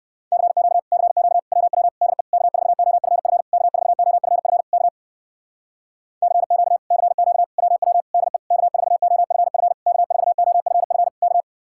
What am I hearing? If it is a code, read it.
Morse code, heard as a keyed tone, about 59 wpm, 700 Hz. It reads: CQ CQ WW DE K4OFF K4OFF K CQ CQ WW DE K4OFF K4OFF K